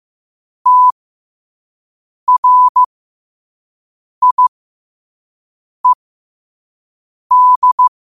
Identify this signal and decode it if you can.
Morse code, heard as a keyed tone, about 15 wpm, spaced as for 5 wpm, 1000 Hz. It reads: TRIED